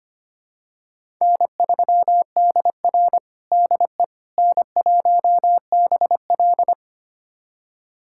Morse code, heard as a keyed tone, about 25 wpm, 700 Hz. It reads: N3DR DE N1BL
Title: USA Callsign DE Callsign - 4 Characters - 25wpm